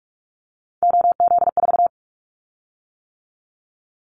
Morse code, heard as a keyed tone, about 46 wpm, 690 Hz. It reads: OZ4
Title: Callsign Prefixes for Top 50 DXCC - 45wpm